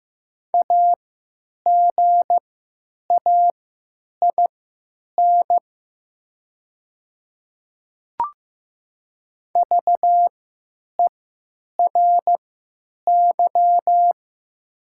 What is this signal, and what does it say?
Morse code, heard as a keyed tone, about 15 wpm, spaced as for 9 wpm, 700 Hz. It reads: AGAIN VERY
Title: 3x Character Spacing - Top 300 Words - 15wpm